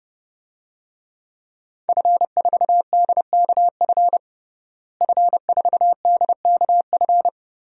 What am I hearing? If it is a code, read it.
Morse code, heard as a keyed tone, about 30 wpm, 700 Hz. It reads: F4DKF F4DKF
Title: DXCC-100 Call Signs - 5 Characters - Repeated 2x - 30wpm